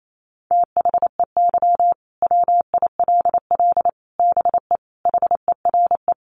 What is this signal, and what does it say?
Morse code, heard as a keyed tone, about 28 wpm, 700 Hz. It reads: THEY WILL BE HERE
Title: Sentences from Top 600 Words 28wpm